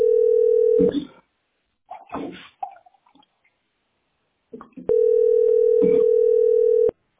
Ringback tone (North America), 440 + 480 Hz.